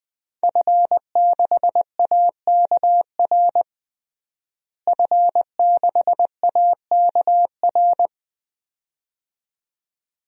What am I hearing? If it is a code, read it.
Morse code, heard as a keyed tone, about 21 wpm, 700 Hz. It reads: F6AKR F6AKR
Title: DXCC-100 Call Signs - 5 Characters - Repeated 2x - 20wpm